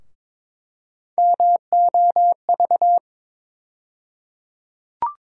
Morse code, heard as a keyed tone, about 22 wpm, 700 Hz. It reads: MOV
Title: USA Call Sign Postfixes - 3 Letters 22wpm